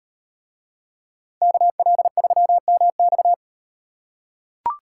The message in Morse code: KL3MX

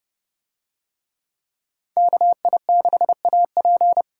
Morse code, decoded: KI6AP

